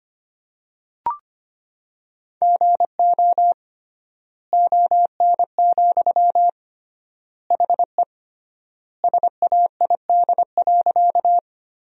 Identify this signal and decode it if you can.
Morse code, heard as a keyed tone, about 25 wpm, 700 Hz. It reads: GO ON, HE SAID.
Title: 3x Word-Spacing - Sentences from Top 100 Words - 25wpm